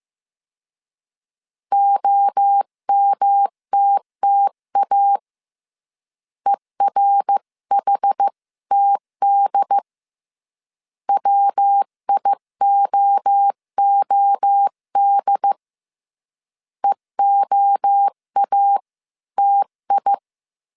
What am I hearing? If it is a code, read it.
Morse code, heard as a keyed tone, about 15 wpm, 800 Hz. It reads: OMTTA ERHTD WIOOD EOATI
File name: CW / DET / 1 3 1500C S C 14wpm 2458